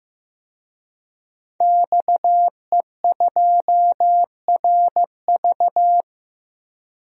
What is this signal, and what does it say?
Morse code, heard as a keyed tone, about 15 wpm, 700 Hz. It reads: XE2RV
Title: DXCC-100 Call Signs - 5 Characters - 15wpm